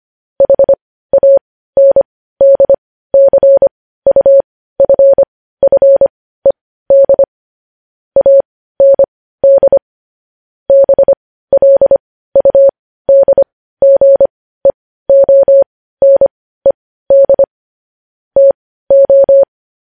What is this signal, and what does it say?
Morse code, heard as a keyed tone, about 25 wpm, spaced as for 15 wpm, 550 Hz. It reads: HANDCUFFED AND BLUDGEONED TO
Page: news-headlines-in-morse-code-at-15-wpm